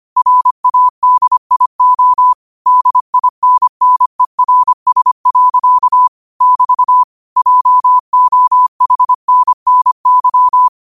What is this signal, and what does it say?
Morse code, heard as a keyed tone, about 25 wpm, 1000 Hz. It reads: RADIO DINNERS. <BT> JOHNNY
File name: qotdfiles / 25wpm